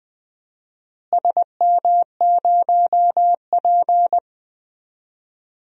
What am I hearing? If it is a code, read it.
Morse code, heard as a keyed tone, about 20 wpm, 700 Hz. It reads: SM0P
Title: DXCC-100 Call Signs - 4 Characters - Repeated 2x - 20wpm